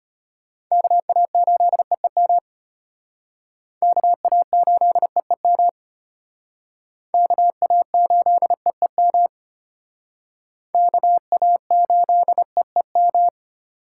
Morse code, heard as a keyed tone, about 32 wpm, 700 Hz. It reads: KA8EEM KA8EEM KA8EEM KA8EEM